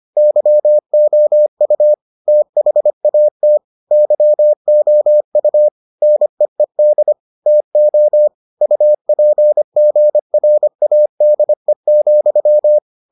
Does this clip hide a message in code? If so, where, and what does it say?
Yes: Morse code from 0.2 to 12.8 s, heard as a keyed tone, about 25 wpm, 600 Hz. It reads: YOU THAT YOU NEED TO UPGRADE,